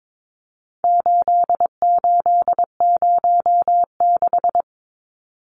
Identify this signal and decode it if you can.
Morse code, heard as a keyed tone, about 22 wpm, 700 Hz. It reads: 8806
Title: Numbers - 4-Digit 22wpm